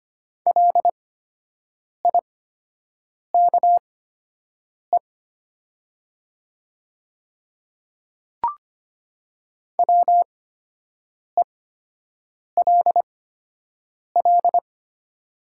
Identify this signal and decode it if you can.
Morse code, heard as a keyed tone, about 25 wpm, spaced as for 7 wpm, 700 Hz. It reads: LIKE WELL
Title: 8x Character Spacing - Top 300 Words - 25wpm